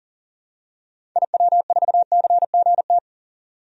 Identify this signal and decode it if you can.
Morse code, heard as a keyed tone, about 40 wpm, 700 Hz. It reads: IW4CGT
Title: DXCC-100 Call Signs - 6 Characters - 40wpm